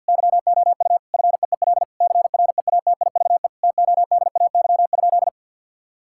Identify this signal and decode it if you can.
Morse code, heard as a keyed tone, about 50 wpm, 700 Hz. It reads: YOU FEEL CREATIVE TODAY?
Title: Sentences from Top 1000 Words - Review Part 5 - 50wpm